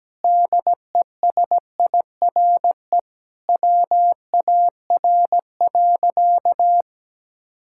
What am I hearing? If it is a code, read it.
Morse code, heard as a keyed tone, about 17 wpm, 700 Hz. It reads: DESIRE WAR.
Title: Sentences from Top 1600 Words 17wpm